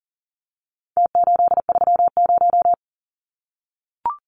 Morse code, heard as a keyed tone, about 40 wpm, 700 Hz. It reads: T830
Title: Numbers - Contest Style - 4-Digit 40wpm